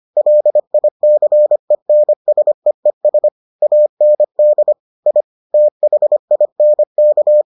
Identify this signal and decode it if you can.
Morse code, heard as a keyed tone, about 25 wpm, 600 Hz. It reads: LICENSEES AND I THINK